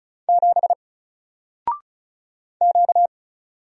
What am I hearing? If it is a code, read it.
Morse code, heard as a keyed tone, about 35 wpm, 700 Hz. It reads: 7 Q
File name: Single-Letter-Number-35wpm